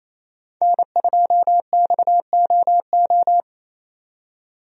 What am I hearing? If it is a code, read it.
Morse code, heard as a keyed tone, about 28 wpm, 700 Hz. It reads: N2XOO
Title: USA Call Signs - 5 Characters - 28wpm